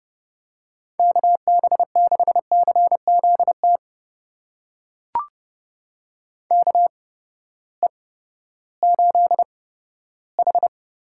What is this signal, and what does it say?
Morse code, heard as a keyed tone, about 32 wpm, spaced as for 15 wpm, 700 Hz. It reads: KB6CZT KE8H
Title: USA Call Signs - 6 Characters - 8x Character Spacing - 30wpm